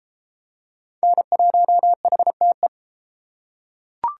Morse code, heard as a keyed tone, about 31 wpm, 700 Hz. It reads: N1HTE